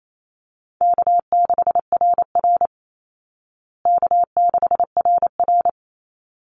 Morse code, heard as a keyed tone, about 28 wpm, 700 Hz. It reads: K6RR K6RR
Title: Easy Call Signs Repeated 2x - 28wpm